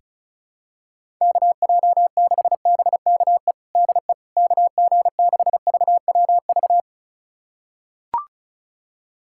Morse code, heard as a keyed tone, about 34 wpm, 700 Hz. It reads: KJ6BKE DE KG6VWV
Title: USA Callsign DE Callsign - 6 Characters - 35wpm